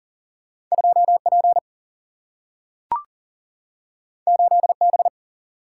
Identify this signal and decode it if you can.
Morse code, heard as a keyed tone, about 40 wpm, 700 Hz. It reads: 2P 8B
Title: Single-Letter Single-Number 40wpm